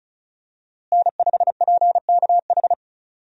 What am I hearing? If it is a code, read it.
Morse code, heard as a keyed tone, about 36 wpm, 700 Hz. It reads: N5PKH
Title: USA Call Signs - 5 Characters - Repeated 2x - 35wpm